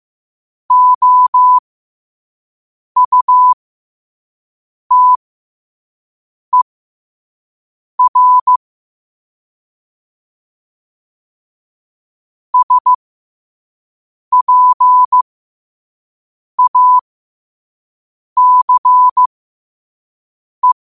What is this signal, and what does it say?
Morse code, heard as a keyed tone, about 15 wpm, spaced as for 5 wpm, 1000 Hz. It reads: OUTER SPACE
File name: qotdfiles / 5wpm